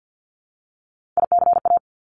Morse code, heard as a keyed tone, about 50 wpm, 700 Hz. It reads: ICA
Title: Top Three Letter Combinations in Words 50wpm